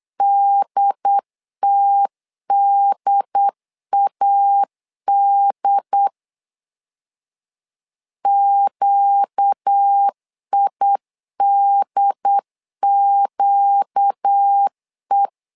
Morse code, heard as a keyed tone, about 8 wpm, 800 Hz. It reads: DTDAD QIDQE
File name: 4 Q 8wpm